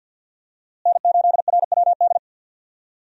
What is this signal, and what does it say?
Morse code, heard as a keyed tone, about 48 wpm, 700 Hz. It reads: N8RWD
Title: USA Call Signs - 5 Characters - Repeated 2x - 50wpm